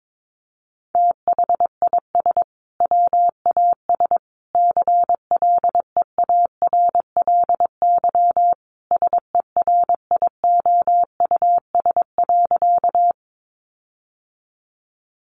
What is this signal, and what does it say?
Morse code, heard as a keyed tone, about 22 wpm, 700 Hz. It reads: THIS WAS CLEARLY SERIOUS.
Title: Sentences from Top 1300 Words 22wpm